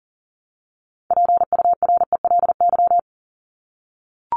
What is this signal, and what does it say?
Morse code, heard as a keyed tone, about 40 wpm, 700 Hz. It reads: PURELY
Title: LY Suffix - Top 100 Words with Base Words 40wpm